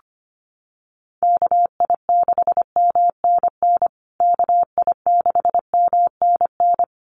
Morse code, heard as a keyed tone, about 25 wpm, 700 Hz. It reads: KI6MNN KI6MNN
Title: USA Call Signs Repeated 2x 25wpm